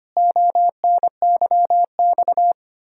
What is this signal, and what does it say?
Morse code, heard as a keyed tone, about 25 wpm, 700 Hz. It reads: ONYX
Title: Pangram Challenge v2 - 25wpm